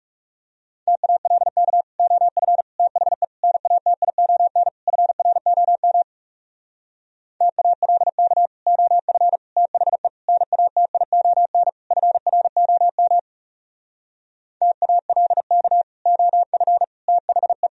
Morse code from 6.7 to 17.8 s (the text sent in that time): TALK OF THE NATION FROM TALK OF THE